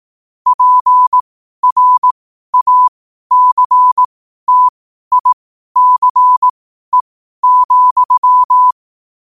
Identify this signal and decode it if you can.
Morse code, heard as a keyed tone, about 18 wpm, spaced as for 12 wpm, 1000 Hz. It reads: PRACTICE,
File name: qotdfiles / 12wpm